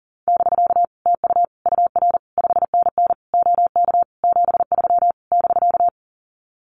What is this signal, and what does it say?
Morse code, heard as a keyed tone, about 40 wpm, 700 Hz. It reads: <BK> TU UR 5NN OK 73 <BK>
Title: Common QSO Phrases Encore 40wpm